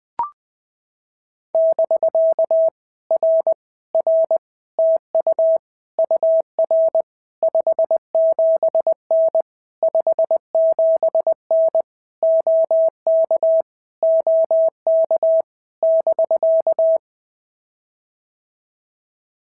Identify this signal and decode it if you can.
Morse code, heard as a keyed tone, about 20 wpm, 650 Hz. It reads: <BK> R R TU UR 57N 57N OK OK <BK>
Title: POTA QSO Practice - 20wpm